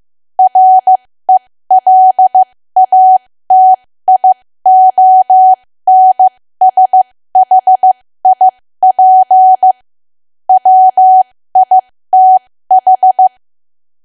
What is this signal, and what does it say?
Morse code, heard as a keyed tone, about 15 wpm, spaced as for 13 wpm, 750 Hz. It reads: RELATIONSHIP WITH
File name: cwa / W1AW-Level2 / 13WPM